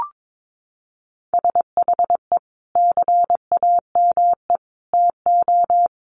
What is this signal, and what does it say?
Morse code, heard as a keyed tone, about 22 wpm, 700 Hz. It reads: SHE CAME TO
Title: Sentences from Top 1100 Words 22wpm